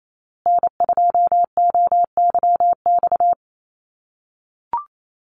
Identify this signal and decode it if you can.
Morse code, heard as a keyed tone, about 28 wpm, 700 Hz. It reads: N2OYX
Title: USA Call Signs Repeated 3x 28wpm